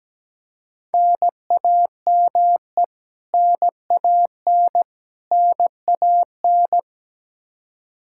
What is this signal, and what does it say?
Morse code, heard as a keyed tone, about 17 wpm, 700 Hz. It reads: NAME NAN NAN
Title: Practice QSO - Exchange 1 - Beginning - 17wpm